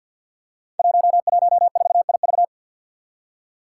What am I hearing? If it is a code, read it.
Morse code, heard as a keyed tone, about 50 wpm, 700 Hz. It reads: 114IV
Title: Five Challenging Letter-Number Combinations 50wpm